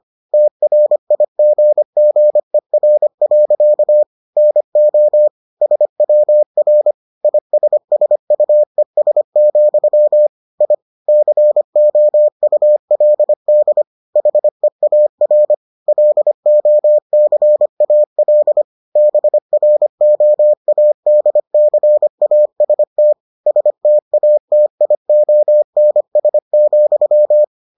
Morse code, heard as a keyed tone, about 25 wpm, 600 Hz. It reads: TRIGGER. NO SWR ISSUES, I COULD HEAR LOCAL BROADCAST STATIONS,